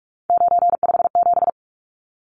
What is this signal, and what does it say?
Morse code, heard as a keyed tone, about 44 wpm, 700 Hz. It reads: 957
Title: Numbers - 3-Digit 45wpm